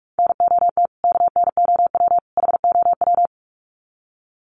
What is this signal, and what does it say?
Morse code, heard as a keyed tone, about 44 wpm, 700 Hz. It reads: NOT KNOW HOW
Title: Sets of 3 Words 45wpm